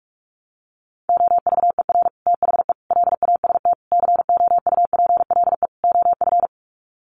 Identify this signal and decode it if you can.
Morse code, heard as a keyed tone, about 44 wpm, 700 Hz. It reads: OVER THE LAST COUPLE OF
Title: Sets of 5 Words Encore 45wpm